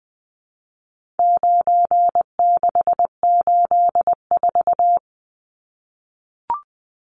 Morse code, heard as a keyed tone, about 20 wpm, 700 Hz. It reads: 9684
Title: Numbers - 4-Digit 20wpm